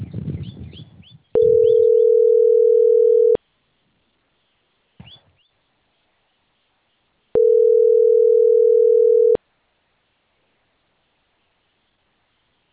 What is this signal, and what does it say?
Ringback tone (North America), 440 + 480 Hz.